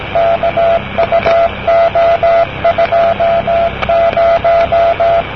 Morse code, heard as a keyed tone, about 17 wpm, 680 Hz, with noise. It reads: KUO20
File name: namerica / usa / usa-usad-cw-kuo20-de-kkn50-rb-1973